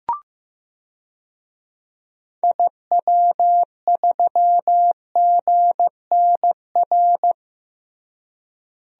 Morse code, heard as a keyed tone, about 15 wpm, 700 Hz. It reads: IW3GNR
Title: DXCC-100 Call Signs - 6 Characters - Repeated 2x - 15wpm